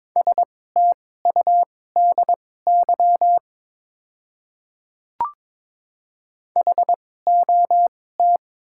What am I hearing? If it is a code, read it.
Morse code, heard as a keyed tone, about 22 wpm, spaced as for 16 wpm, 700 Hz. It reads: STUDY HOT
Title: 2x Character Spacing - Top 300 Words - 22wpm